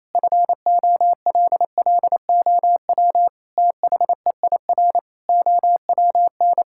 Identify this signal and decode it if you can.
Morse code, heard as a keyed tone, about 28 wpm, 700 Hz. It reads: FOLLOW THEIR OWN
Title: Sentences from Top 1000 Words - Review Part 6 - 28wpm